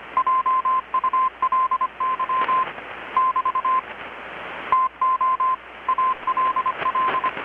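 Morse code, heard as a keyed tone, about 25 wpm, 1040 Hz, with radio noise. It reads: JULY <BT> TO ALL
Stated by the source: Morse from KPH's Night of Nights XX maritime radiotelegraphy event, received in Chiapas, Mexico about 3,800 km away on a Kenwood R-600 with a grounded 10 m long-wire antenna.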